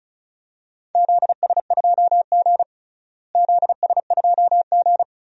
Morse code, heard as a keyed tone, about 34 wpm, 700 Hz. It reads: ZS2G ZS2G
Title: DXCC-100 Call Signs - 4 Characters - Repeated 3x - 35wpm